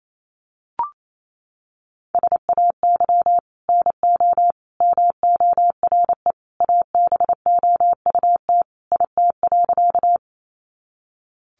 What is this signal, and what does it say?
Morse code, heard as a keyed tone, about 28 wpm, 700 Hz. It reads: SAY NO MORE ABOUT IT.
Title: Sentences from Top 200 Words 28wpm